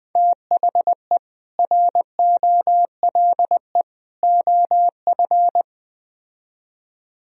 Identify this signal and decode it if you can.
Morse code, heard as a keyed tone, about 20 wpm, 700 Hz. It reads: THE ROLE OF
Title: Speed-Racing - Sets of 3 Words 17wpm